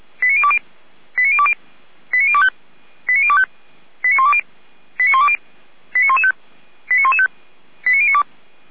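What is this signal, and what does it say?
Five-tone selective call (ZVEI), heard as quick runs of five tones, tones of about 70 ms. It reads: 89020 89020 89025 89025 89120 89120 89115 89115 89092
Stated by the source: SDR-received ZVEI five-tone selective call, address 89270.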